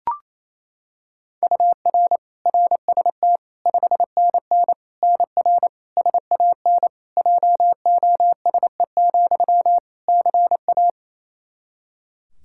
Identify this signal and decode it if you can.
Morse code, heard as a keyed tone, about 28 wpm, 700 Hz. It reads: UR RST 5NN NR SAN JOSE, CA
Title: Practice QSO - Exchange 1 - Beginning - 28wpm